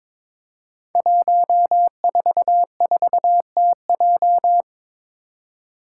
Morse code, heard as a keyed tone, about 23 wpm, 700 Hz. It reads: 144TJ
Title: Five Challenging Letter-Number Combinations 22wpm